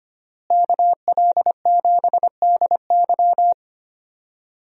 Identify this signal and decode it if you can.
Morse code, heard as a keyed tone, about 25 wpm, 700 Hz. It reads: KL7DY
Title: DXCC-100 Call Signs - 5 Characters - Repeated 2x - 25wpm